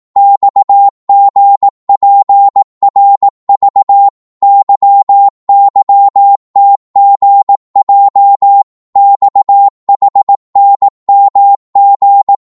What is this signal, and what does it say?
Morse code, heard as a keyed tone, about 18 wpm, 800 Hz. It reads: XGPRV YYTGJ XHNMG
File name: Morse / ltr-18wpm